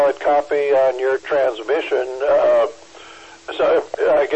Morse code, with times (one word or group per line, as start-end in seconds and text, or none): none